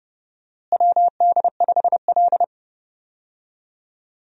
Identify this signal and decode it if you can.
Morse code, heard as a keyed tone, about 30 wpm, 700 Hz. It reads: WD5L
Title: USA Call Signs - 4 Characters - Repeated 2x - 30wpm